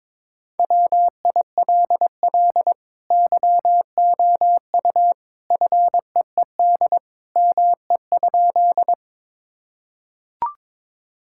Morse code, heard as a keyed tone, about 22 wpm, 700 Hz. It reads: WILL YOU FEED ME?
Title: Sentences from Top 1600 Words 22wpm